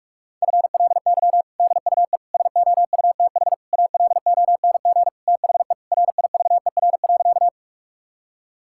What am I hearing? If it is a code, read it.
Morse code, heard as a keyed tone, about 46 wpm, 700 Hz. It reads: FLY DUE SOUTH ALONG THE RIVER.
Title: Sentences from Top 1000 Words - Review Part 6 - 45wpm